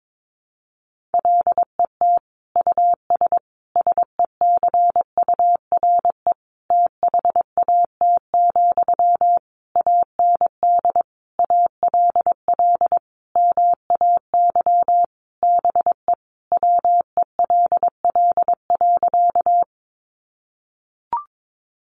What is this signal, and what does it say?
Morse code, heard as a keyed tone, about 22 wpm, 700 Hz. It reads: LET US SECURE THAT, AND ALL MAY BE WELL.
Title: Sentences from Top 1200 Words 22wpm